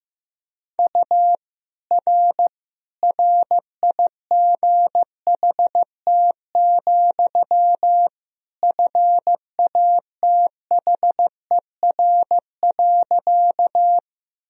Morse code, heard as a keyed tone, about 15 wpm, 700 Hz. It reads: U R RIGHT, FATHER.
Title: Sentences from Top 300 Words and Abbreviations 15wpm